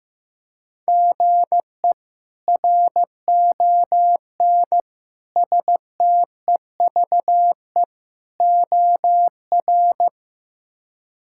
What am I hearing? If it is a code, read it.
Morse code, heard as a keyed tone, about 16 wpm, 700 Hz. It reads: GE RON STEVE OR
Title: K1USN SST QSO Practice - 15wpm